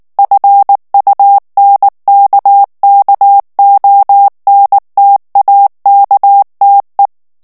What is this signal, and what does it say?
Morse code, heard as a keyed tone, about 19 wpm, 800 Hz. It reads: FUNKKONTAKTE
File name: morse mp3 / ALLEMAND texte clair 2 vitesse 18 mots minute